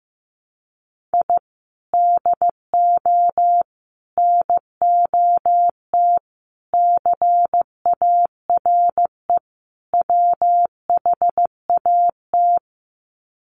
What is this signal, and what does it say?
Morse code, heard as a keyed tone, about 15 wpm, 700 Hz. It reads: I DO NOT CARE WHAT
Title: Sets of 5 Words 15wpm